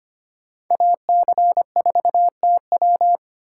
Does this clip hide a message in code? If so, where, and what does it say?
Yes: Morse code from 0.7 to 3.2 s, heard as a keyed tone, about 32 wpm, 700 Hz. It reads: AC4TW